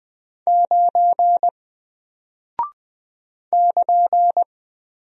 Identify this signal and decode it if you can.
Morse code, heard as a keyed tone, about 20 wpm, 700 Hz. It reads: 9 <KN>
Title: Single Character (Letter:Number:Punctuation:Prosign) Alt 20wpm